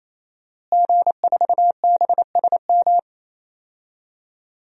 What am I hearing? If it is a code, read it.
Morse code, heard as a keyed tone, about 28 wpm, 700 Hz. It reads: G4BSM
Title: DXCC-100 Call Signs - 5 Characters - Repeated 3x - 28wpm